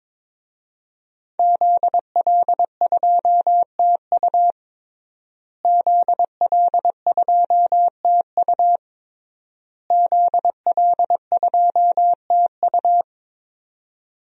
Morse code, heard as a keyed tone, about 21 wpm, 700 Hz. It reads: ZL2TU ZL2TU ZL2TU
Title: DXCC-100 Call Signs - 5 Characters - Repeated 3x - 22wpm